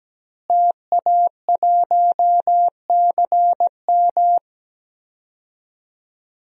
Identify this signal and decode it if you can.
Morse code, heard as a keyed tone, about 17 wpm, 700 Hz. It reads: TA1CM
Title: DXCC-100 Call Signs - 5 Characters - Repeated 2x - 17wpm